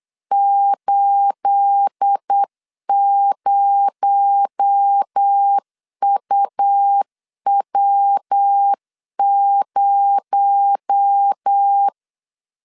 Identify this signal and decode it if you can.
Morse code, heard as a keyed tone, about 8 wpm, 800 Hz. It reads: 80UW0
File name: CW / DET / 1 9 1500C S A 8wpm 5637